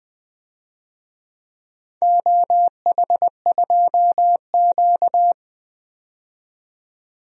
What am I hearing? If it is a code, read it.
Morse code, heard as a keyed tone, about 20 wpm, 700 Hz. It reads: OH2Q
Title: DXCC-100 Call Signs - 4 Characters - 20wpm